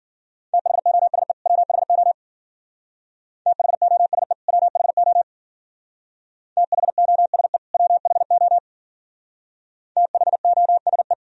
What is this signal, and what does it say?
Morse code, heard as a keyed tone, about 50 wpm, 700 Hz. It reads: THOSE WHO THOSE WHO THOSE WHO THOSE